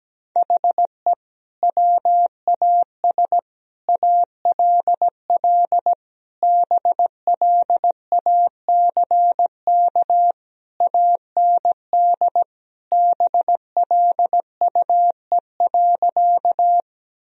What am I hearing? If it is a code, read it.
Morse code, heard as a keyed tone, about 17 wpm, 700 Hz. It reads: HE WAS ALL BLACK AND BLUE.
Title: Sentences from Top 1000 Words - Review Part 3 - 17wpm